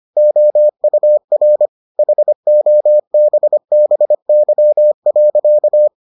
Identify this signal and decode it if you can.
Morse code, heard as a keyed tone, about 25 wpm, 600 Hz. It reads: OUR HOBBY.